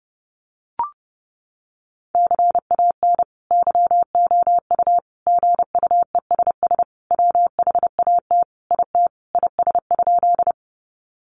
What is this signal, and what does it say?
Morse code, heard as a keyed tone, about 30 wpm, 700 Hz. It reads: CAN YOU GUESS WHAT IT IS?
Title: Sentences from Top 1500 Words 30wpm